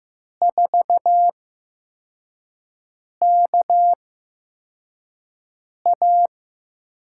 Morse code, heard as a keyed tone, about 15 wpm, spaced as for 4 wpm, 700 Hz. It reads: 4KA
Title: USA Call Signs - 6 Characters - 8x Character Spacing - 15wpm